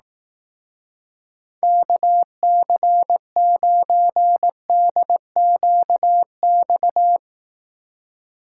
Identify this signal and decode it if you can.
Morse code, heard as a keyed tone, about 18 wpm, 700 Hz. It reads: KC9DQX
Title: Speed Racing - USA 2x3 Call Signs 15wpm